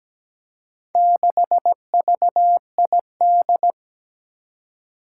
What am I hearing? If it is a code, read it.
Morse code, heard as a keyed tone, about 17 wpm, 700 Hz. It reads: 6VID
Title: Four Challenging Letter-Number Combinations 17wpm